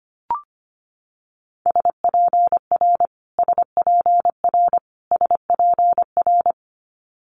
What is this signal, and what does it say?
Morse code, heard as a keyed tone, about 25 wpm, 700 Hz. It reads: SPR SPR SPR
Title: Multi-letter Phonemes Repeated 3x 25wpm